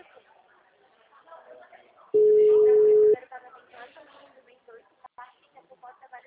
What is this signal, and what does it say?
Ringback tone (Europe, 425 Hz), with speech.